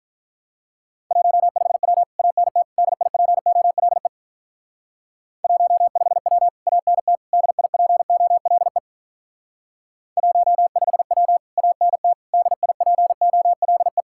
Morse code, heard as a keyed tone, about 44 wpm, 700 Hz. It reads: 15W ANT DIPOLE 15W ANT DIPOLE 15W ANT DIPOLE